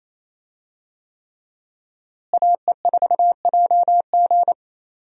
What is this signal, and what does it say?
Morse code, heard as a keyed tone, about 28 wpm, 700 Hz. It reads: AE4JG
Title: USA Call Signs - 5 Characters - Repeated 3x - 28wpm